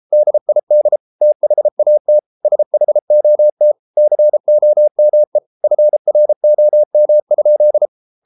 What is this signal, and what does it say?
Morse code, heard as a keyed tone, about 33 wpm, 600 Hz. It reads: DID THAT SHOT COME FROM?